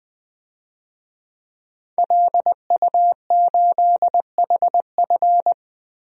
Morse code, heard as a keyed tone, about 21 wpm, 700 Hz. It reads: LU8HF
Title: DXCC-100 Call Signs - 5 Characters - 20wpm